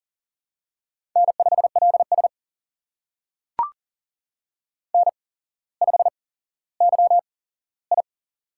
Morse code, heard as a keyed tone, about 39 wpm, spaced as for 17 wpm, 700 Hz. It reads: N5LS N5YI
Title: USA Call Signs - 4 Characters - 8x Character Spacing - 40wpm